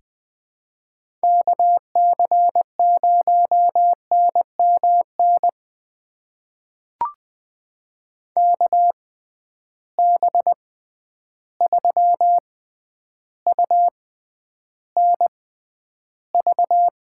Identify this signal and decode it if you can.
Morse code, heard as a keyed tone, about 20 wpm, spaced as for 10 wpm, 700 Hz. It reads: KC0NMN KB3UNV
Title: USA Call Signs - 6 Characters - 6x Character Spacing - 20wpm